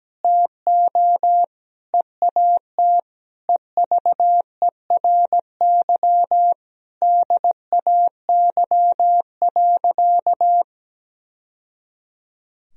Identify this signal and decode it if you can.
Morse code, heard as a keyed tone, about 17 wpm, 700 Hz. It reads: TO EAT EVERY DAY.
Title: Sentences from Top 1000 Words - Review Part 4 - 17wpm